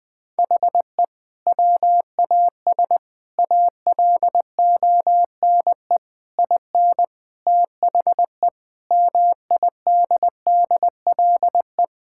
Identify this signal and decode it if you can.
Morse code, heard as a keyed tone, about 21 wpm, 700 Hz. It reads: HE WAS ALONE IN THE MIDDLE
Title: Sentences from Top 1000 Words - Review Part 6 - 20wpm